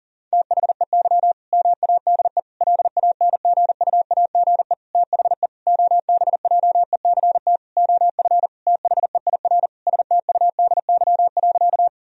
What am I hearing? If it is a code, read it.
Morse code, heard as a keyed tone, about 40 wpm, 700 Hz. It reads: THEY MADE LANGUAGE THE OBJECT OF THEIR STUDY.
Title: Sentences from Top 1000 Words - Review Part 3 - 40wpm